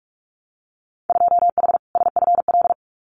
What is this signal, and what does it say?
Morse code, heard as a keyed tone, about 46 wpm, 700 Hz. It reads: 2H SFL
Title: Winter Field Day Exchanges Encore 45wpm